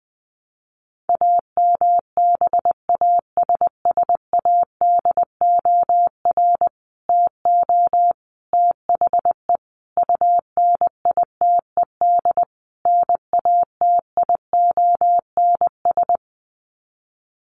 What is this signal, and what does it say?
Morse code, heard as a keyed tone, about 20 wpm, 700 Hz. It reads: AMBASSADOR TO THE UNITED NATIONS
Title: Sets of 5 Words Encore2 20wpm